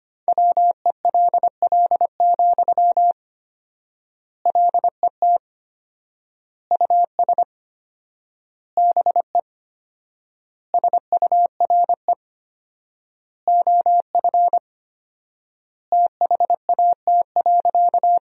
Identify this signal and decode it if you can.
Morse code, heard as a keyed tone, about 25 wpm, 700 Hz. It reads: WELL, LET US BE SURE OF THAT.
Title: 4x Word-Spacing - Sentences from Top 300 Words - 25wpm